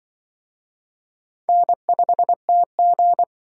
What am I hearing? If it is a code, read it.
Morse code, heard as a keyed tone, about 24 wpm, 700 Hz. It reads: N5TG
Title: Speed Racing - USA 1x2 Call Signs 20wpm